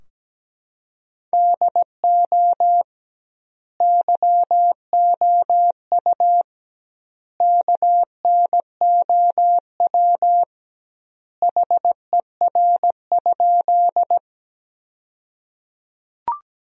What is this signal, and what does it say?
Morse code, heard as a keyed tone, about 17 wpm, 700 Hz. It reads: DO YOU KNOW HER?
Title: 2x Word-Spacing - Sentences from Top 100 Words - 17wpm